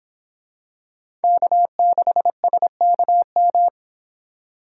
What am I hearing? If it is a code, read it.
Morse code, heard as a keyed tone, about 25 wpm, 700 Hz. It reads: K6SKM